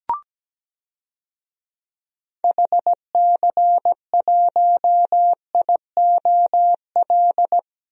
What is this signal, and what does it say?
Morse code, heard as a keyed tone, about 17 wpm, 700 Hz. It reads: HC1IOL
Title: DXCC-100 Call Signs - 6 Characters - 17wpm